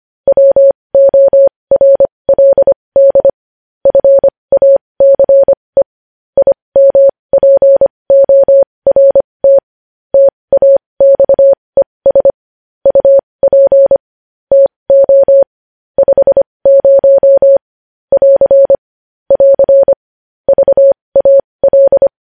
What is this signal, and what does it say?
Morse code, heard as a keyed tone, about 25 wpm, spaced as for 20 wpm, 550 Hz. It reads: WORLD FACE IMPORT TAXES UP TO 50 <AR> <AR> VAL
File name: podcast / News headlines-1x-25x20